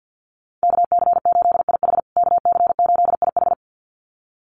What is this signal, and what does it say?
Morse code, heard as a keyed tone, about 50 wpm, 700 Hz. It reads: KC8IH KC8IH
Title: USA Call Signs Repeated 2x 50wpm